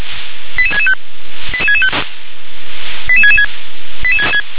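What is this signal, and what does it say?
Five-tone selective call (ZVEI), heard as quick runs of five tones, tones of about 70 ms. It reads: 88665 88665 88666
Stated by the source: ZVEI five-tone call received by SDR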